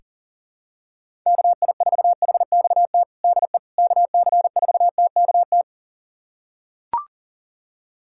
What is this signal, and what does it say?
Morse code, heard as a keyed tone, about 40 wpm, 700 Hz. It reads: KI4HXT DE KC4TKT
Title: USA Callsign DE Callsign - 6 Characters - 40wpm